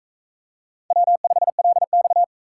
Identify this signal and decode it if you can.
Morse code, heard as a keyed tone, about 37 wpm, 700 Hz. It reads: W5LX